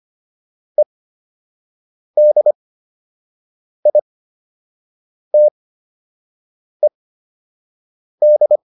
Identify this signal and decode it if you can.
Morse code, heard as a keyed tone, about 25 wpm, spaced as for 6 wpm, 600 Hz. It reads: EDITED